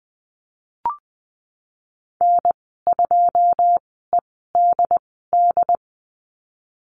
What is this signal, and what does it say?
Morse code, heard as a keyed tone, about 20 wpm, spaced as for 15 wpm, 700 Hz. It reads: N2EDD
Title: 088 Lesson 15g - Number 5 - Call Signs 20wpm